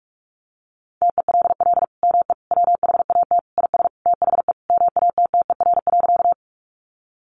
Morse code, heard as a keyed tone, about 44 wpm, 700 Hz. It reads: TELL ME WHAT IS THE MATTER.
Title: Sentences from Top 700 Words 45wpm